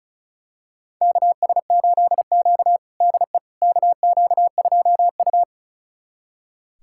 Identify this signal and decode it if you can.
Morse code, heard as a keyed tone, about 36 wpm, 700 Hz. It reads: KS8Q DE KQ2U
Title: USA Callsign DE Callsign - 4 Characters - 35wpm